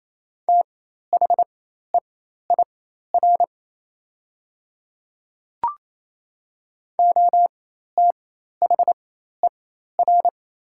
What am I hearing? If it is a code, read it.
Morse code, heard as a keyed tone, about 28 wpm, spaced as for 13 wpm, 690 Hz. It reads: THEIR OTHER
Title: 4x Character Spacing - Top 500 Words - 28wpm